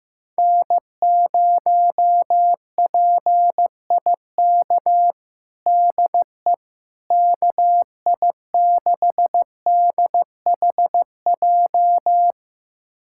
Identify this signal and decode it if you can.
Morse code, heard as a keyed tone, about 15 wpm, 700 Hz. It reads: N0PIK DE KI6DHJ
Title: USA Callsign DE Callsign 15wpm